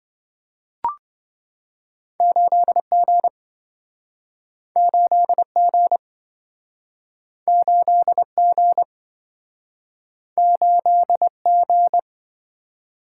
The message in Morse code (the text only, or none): 8G 8G 8G 8G